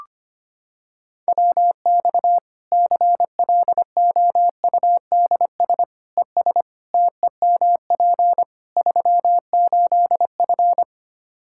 Morse code, heard as a keyed tone, about 25 wpm, 700 Hz. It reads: WX CLOUDS ES TEMP 38F
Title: Practice QSO - Exchange 2 - Info - 25wpm